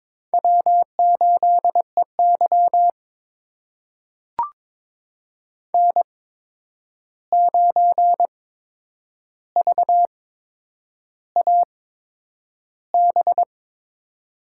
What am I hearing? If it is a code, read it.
Morse code, heard as a keyed tone, about 22 wpm, spaced as for 9 wpm, 700 Hz. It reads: W8EY N9VAB
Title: USA Call Signs - 8x Character Spacing - 22wpm